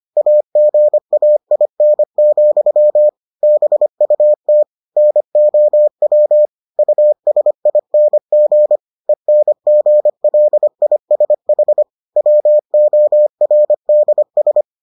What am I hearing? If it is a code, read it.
Morse code, heard as a keyed tone, about 25 wpm, 600 Hz. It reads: AGAIN, BUT NOW USING ENGLISH WORDS